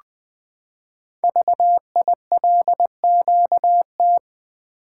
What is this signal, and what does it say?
Morse code, heard as a keyed tone, about 20 wpm, 700 Hz. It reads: VILQT
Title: Code Groups - 5 Letters - 20wpm